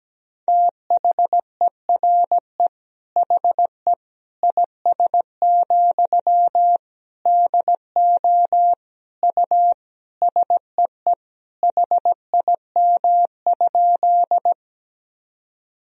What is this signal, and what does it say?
Morse code, heard as a keyed tone, about 17 wpm, 700 Hz. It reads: THERE HE IS, DO U SEE HIM?
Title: Sentences from Top 100 Words and Abbreviations 17wpm